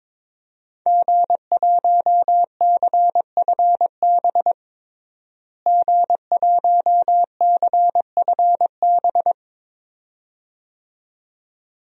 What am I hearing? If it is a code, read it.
Morse code, heard as a keyed tone, about 22 wpm, 700 Hz. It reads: G1CFB G1CFB
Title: DXCC-100 Call Signs - 5 Characters - Repeated 2x - 22wpm